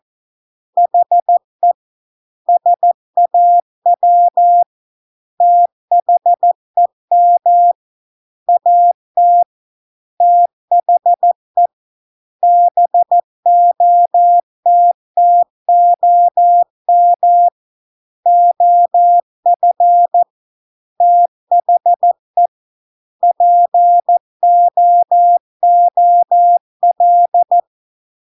Morse code, heard as a keyed tone, about 14 wpm, 700 Hz. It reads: HE SAW THEM AT THE BOTTOM OF THE POOL